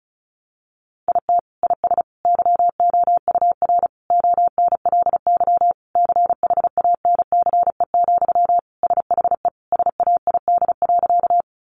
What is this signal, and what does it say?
Morse code, heard as a keyed tone, about 34 wpm, 700 Hz. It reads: IT IS YOUR ONLY CHANCE, SHE SAID.
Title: Sentences from Top 800 Words 35wpm